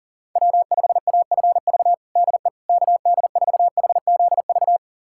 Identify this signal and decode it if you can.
Morse code, heard as a keyed tone, about 37 wpm, 700 Hz. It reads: W5AFV DE KD4HZV
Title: USA Callsign DE Callsign 40wpm